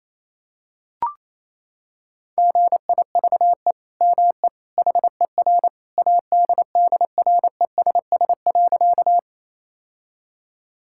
Morse code, heard as a keyed tone, about 28 wpm, 700 Hz. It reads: GIVE ME HER ADDRESS.
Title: Sentences from Top 1000 Words - Review Part 6 - 28wpm